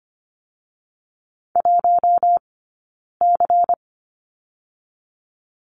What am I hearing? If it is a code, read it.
Morse code, heard as a keyed tone, about 25 wpm, 700 Hz. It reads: 1 C
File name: course / 25:10wpm / 099 Lesson 17d - Letter u - Two Characters 25wpm